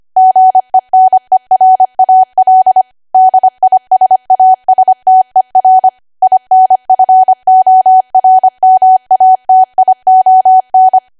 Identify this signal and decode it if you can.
Morse code, heard as a keyed tone, about 25 wpm, 750 Hz. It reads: GENERAL DISASTER INFORMATION
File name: Archive / 25 WPM / 25WPM